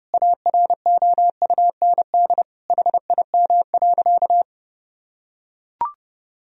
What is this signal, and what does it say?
Morse code, heard as a keyed tone, about 30 wpm, 700 Hz. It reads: AROUND HIM.
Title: Sentences from Top 1000 Words - Review Part 3 - 30wpm